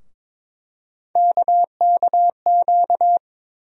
Morse code, heard as a keyed tone, about 21 wpm, 700 Hz. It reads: KKQ
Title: USA Call Sign Postfixes - 3 Letters 22wpm